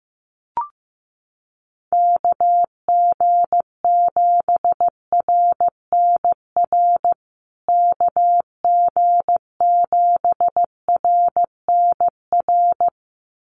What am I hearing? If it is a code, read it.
Morse code, heard as a keyed tone, about 15 wpm, 700 Hz. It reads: KG7RNR KG7RNR
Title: USA Call Signs Repeated 2x 15wpm